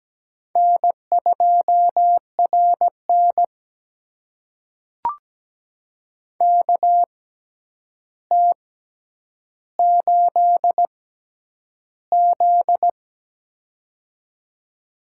Morse code, heard as a keyed tone, about 17 wpm, spaced as for 9 wpm, 700 Hz. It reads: N2RN KT8Z
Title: USA Call Signs - 4 Characters - 6x Character Spacing - 17wpm